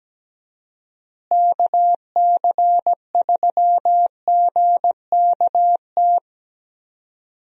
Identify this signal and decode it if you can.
Morse code, heard as a keyed tone, about 17 wpm, 700 Hz. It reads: KC3GKT
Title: USA Call Signs - 6 Characters - Repeated 3x - 17wpm